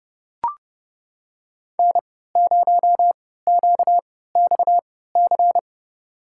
Morse code, heard as a keyed tone, about 30 wpm, spaced as for 17 wpm, 700 Hz. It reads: N0QXC
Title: USA Call Signs - 5 Characters - 3x Character Spacing - 30wpm